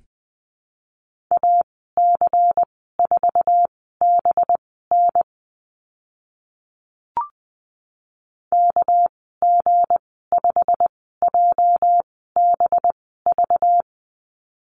Morse code, heard as a keyed tone, about 20 wpm, spaced as for 14 wpm, 700 Hz. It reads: AC4BN KG5JBV
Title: 207 Lesson 33g - Letter j - Call Signs 20wpm